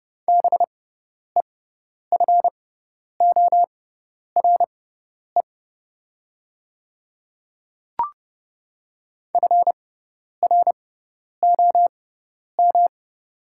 Morse code, heard as a keyed tone, about 30 wpm, spaced as for 10 wpm, 700 Hz. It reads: BEFORE FROM
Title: 6x Character Spacing - Top 300 Words - 30wpm